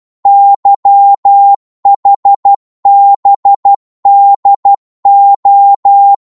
Morse code, heard as a keyed tone, about 12 wpm, 800 Hz. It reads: YHBDO
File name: Morse / ltr-12wpm